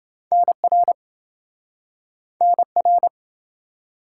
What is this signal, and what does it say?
Morse code, heard as a keyed tone, about 26 wpm, 700 Hz. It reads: NR NR